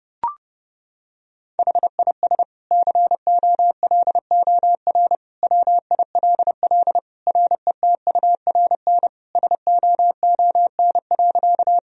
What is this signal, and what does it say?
Morse code, heard as a keyed tone, about 32 wpm, 700 Hz. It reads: HIS COLOR WILL RETURN SOON.
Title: Sentences from Top 1000 Words - Review Part 2 - 30wpm